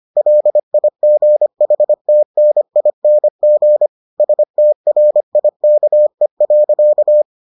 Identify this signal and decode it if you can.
Morse code, heard as a keyed tone, about 25 wpm, 600 Hz. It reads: LIGHTNING STRIKE.